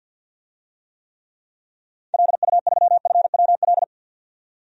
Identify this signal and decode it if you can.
Morse code, heard as a keyed tone, about 53 wpm, 700 Hz. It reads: LU3FWL
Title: DXCC-100 Call Signs - 6 Characters - 50wpm